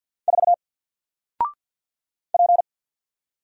Morse code, heard as a keyed tone, about 48 wpm, 700 Hz. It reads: 4 P
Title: Single Letter-Number Alt 50wpm